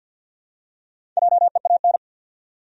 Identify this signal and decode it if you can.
Morse code, heard as a keyed tone, about 53 wpm, 700 Hz. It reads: JEAN
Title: USA Common Names - Max 4 Letters - 50wpm